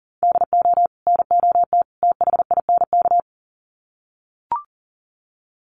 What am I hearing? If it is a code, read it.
Morse code, heard as a keyed tone, about 40 wpm, 700 Hz. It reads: DO NOT THINK
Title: Sets of 3 Words 40wpm